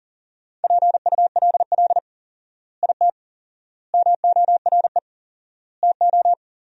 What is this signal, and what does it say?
Morse code, heard as a keyed tone, about 40 wpm, 700 Hz. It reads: PULL IT MORE TO
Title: 4x Word-Spacing - Sentences from Top 400 Words - 40wpm